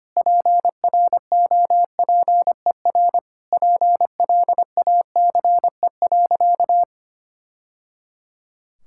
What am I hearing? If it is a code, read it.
Morse code, heard as a keyed tone, about 25 wpm, 700 Hz. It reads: PROPER PLACE.
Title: Sentences from Top 1000 Words - Review Part 5 - 25wpm